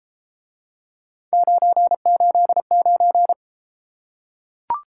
Morse code, heard as a keyed tone, about 33 wpm, 700 Hz. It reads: 989